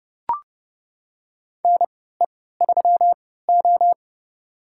Morse code, heard as a keyed tone, about 31 wpm, spaced as for 17 wpm, 700 Hz. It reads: NE3O
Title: USA Call Signs - 4 Characters - 3x Character Spacing - 30wpm